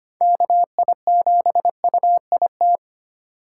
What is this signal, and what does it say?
Morse code, heard as a keyed tone, about 25 wpm, 700 Hz. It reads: KI7UIT
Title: USA Call Signs - 6 Characters - 3x Character Spacing - 25wpm